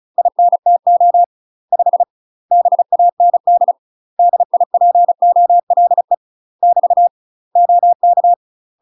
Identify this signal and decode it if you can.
Morse code, heard as a keyed tone, about 36 wpm, 700 Hz. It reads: INTO 5 BAND DIPOLE <BT> OK
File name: UK MASTER QSOs TRAINING / 35WPM / UK MASTER QSOs 0010